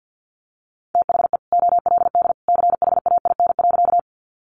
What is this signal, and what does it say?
Morse code, heard as a keyed tone, about 50 wpm, 700 Hz. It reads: THE OLD CHAIN.
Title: Sentences from Top 1500 Words 50wpm